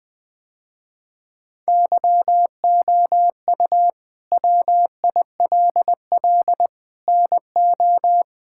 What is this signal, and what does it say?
Morse code, heard as a keyed tone, about 20 wpm, 700 Hz. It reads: YOU WILL NO
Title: Sentences with 6-Word Phrases - Spoken Beginning - 20wpm